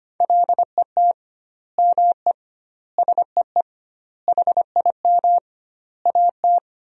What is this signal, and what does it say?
Morse code, heard as a keyed tone, about 25 wpm, 700 Hz. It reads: LET ME SEE HIM AT
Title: 2x Word-Spacing - Sentences from Top 300 Words - 25wpm